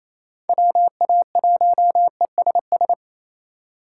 Morse code, heard as a keyed tone, about 28 wpm, 700 Hz. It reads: WA1ESS
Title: USA Call Signs - 2x Character Spacing - 28wpm